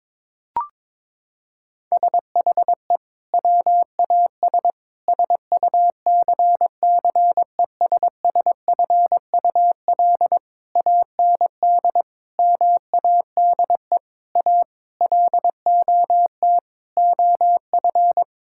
Morse code, heard as a keyed tone, about 22 wpm, 700 Hz. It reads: SHE WAS SUCCESSFUL AND MADE A LOT OF
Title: Sentences from Top 1000 Words - Review Part 6 - 22wpm